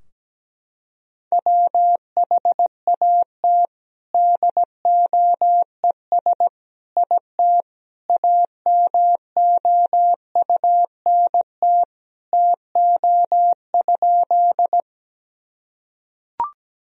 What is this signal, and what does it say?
Morse code, heard as a keyed tone, about 17 wpm, 700 Hz. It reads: WHAT DOES IT AMOUNT TO?
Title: Sentences from Top 1000 Words - Review Part 4 - 17wpm